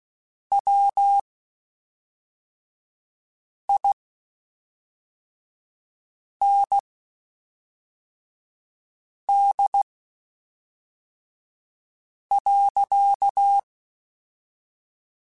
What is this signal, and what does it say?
Morse code, heard as a keyed tone, about 16 wpm, spaced as for 3 wpm, 800 Hz. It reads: WIND.
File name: Spaced Characters / Spaced Characters-mp3s / Weather 3 17 sp @5